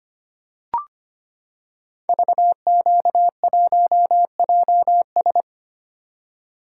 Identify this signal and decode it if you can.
Morse code, heard as a keyed tone, about 25 wpm, 700 Hz. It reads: VQ1JS
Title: Code Groups - 5 Letter:Numbers - 25wpm